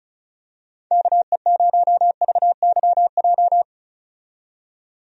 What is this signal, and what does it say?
Morse code, heard as a keyed tone, about 34 wpm, 700 Hz. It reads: KE0VYJ
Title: USA Call Signs - 6 Characters - 35wpm